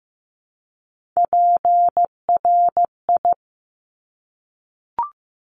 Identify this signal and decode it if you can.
Morse code, heard as a keyed tone, about 15 wpm, 700 Hz. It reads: PRI
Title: Top Three Letter Combinations in Words 15wpm